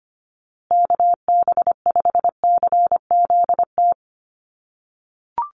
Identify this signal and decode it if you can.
Morse code, heard as a keyed tone, about 25 wpm, 700 Hz. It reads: KB5CZT
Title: USA Call Signs Repeated 2x 25wpm